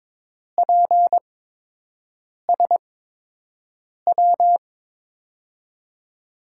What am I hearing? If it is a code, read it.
Morse code, heard as a keyed tone, about 22 wpm, spaced as for 6 wpm, 700 Hz. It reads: PSW
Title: USA Call Signs - 8x Character Spacing - 22wpm